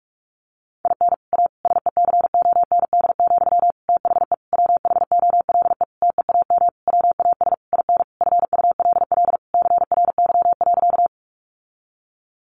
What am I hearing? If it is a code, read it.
Morse code, heard as a keyed tone, about 44 wpm, 700 Hz. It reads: IN A SECOND, THE WHOLE TEAM WAS IN FULL CRY.
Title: Sentences from Top 1500 Words 45wpm